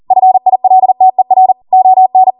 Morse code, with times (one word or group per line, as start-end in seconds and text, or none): 0.1-1.5 FILTER
1.7-2.3 ON